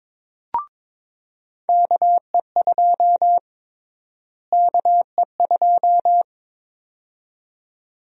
Morse code, heard as a keyed tone, about 22 wpm, 700 Hz. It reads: KE2 KE2
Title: USA Call Sign Prefixes - Repeated 2x - 22wpm